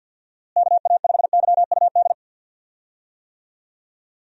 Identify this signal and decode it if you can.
Morse code, heard as a keyed tone, about 53 wpm, 700 Hz. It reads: KA5YUD
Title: USA Call Signs - 6 Characters - Repeated 2x - 50wpm